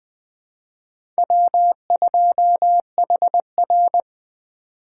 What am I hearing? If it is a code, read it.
Morse code, heard as a keyed tone, about 19 wpm, 700 Hz. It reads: W2HR